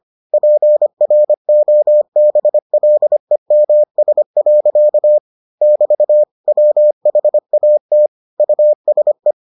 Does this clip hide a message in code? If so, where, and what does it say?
Yes: Morse code from 0.3 to 9.3 s, heard as a keyed tone, about 25 wpm, 600 Hz. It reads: PROBLEMS. <BT> WHAT USE